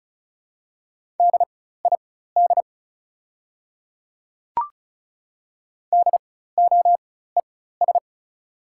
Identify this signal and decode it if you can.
Morse code, heard as a keyed tone, about 36 wpm, spaced as for 16 wpm, 690 Hz. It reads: DID DOES
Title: 4x Character Spacing - Top 500 Words - 35wpm